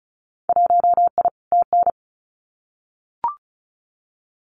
Morse code, heard as a keyed tone, about 34 wpm, 700 Hz. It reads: 1I TN
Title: Winter Field Day Exchanges Encore 35wpm